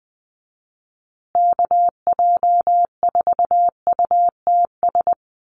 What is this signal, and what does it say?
Morse code, heard as a keyed tone, about 20 wpm, 700 Hz. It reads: KJ4UTS
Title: USA Call Signs Repeated 3x 20wpm